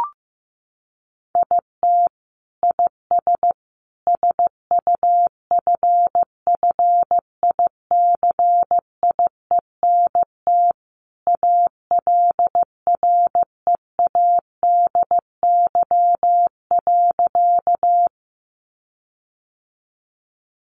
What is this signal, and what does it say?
Morse code, heard as a keyed tone, about 15 wpm, 700 Hz. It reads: IT IS SUFFICIENT ALREADY.
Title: Sentences from Top 1400 Words 15wpm